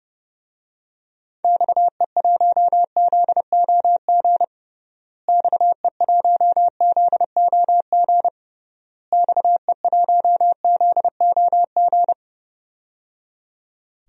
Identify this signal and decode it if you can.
Morse code, heard as a keyed tone, about 30 wpm, 700 Hz. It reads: XE1ZOG XE1ZOG XE1ZOG
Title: DXCC-100 Call Signs - 6 Characters - Repeated 3x - 30wpm